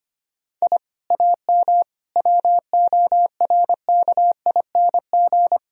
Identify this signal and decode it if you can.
Morse code, heard as a keyed tone, about 25 wpm, 700 Hz. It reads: I AM WORKING
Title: Sentences from Top 1000 Words - Review Part 4 - 25wpm